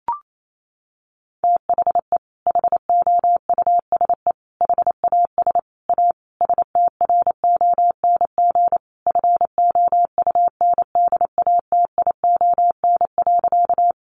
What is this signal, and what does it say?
Morse code, heard as a keyed tone, about 28 wpm, 700 Hz. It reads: THE HOUSE HAS A STRONG FOUNDATION.
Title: Sentences from Top 1200 Words 28wpm